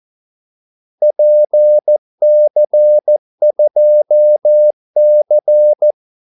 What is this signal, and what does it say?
Morse code, heard as a keyed tone, about 14 wpm, 600 Hz. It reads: PC2C
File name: Callsigns-14wpm